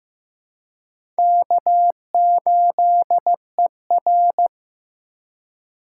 Morse code, heard as a keyed tone, about 15 wpm, 700 Hz. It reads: K8ER
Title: USA Call Signs - 4 Characters - 3x Character Spacing - 15wpm